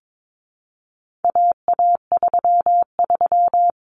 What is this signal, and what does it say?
Morse code, heard as a keyed tone, about 22 wpm, 700 Hz. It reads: AA33
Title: Common Cut Numbers - 4-Digit 22wpm